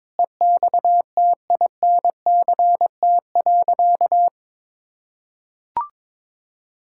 Morse code, heard as a keyed tone, about 22 wpm, 700 Hz. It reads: EXTINCT.
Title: Pangram Challenge v2 - 22wpm